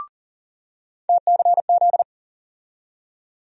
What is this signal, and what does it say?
Morse code, heard as a keyed tone, about 40 wpm, 700 Hz. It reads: TCZ
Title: USA Call Sign Postfixes - 3 Letters 40wpm